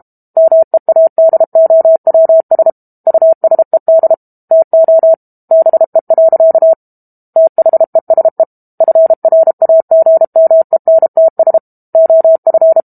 Morse code, heard as a keyed tone, about 32 wpm, 650 Hz. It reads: MEADOWS USED TO BE. THESE FRAGMENTS OF
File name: Book Two / 06-Chapter 6 - 32 WPM